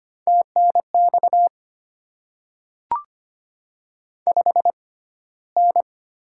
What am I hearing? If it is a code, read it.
Morse code, heard as a keyed tone, about 25 wpm, spaced as for 15 wpm, 700 Hz. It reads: TNX 5N
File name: Top 20 QSO Elements - 6x Character Spacing - 25wpm